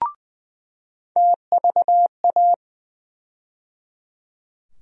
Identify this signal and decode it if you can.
Morse code, heard as a keyed tone, about 20 wpm, 700 Hz. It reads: TVA
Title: USA Call Sign Postfixes - 3 Letters 20wpm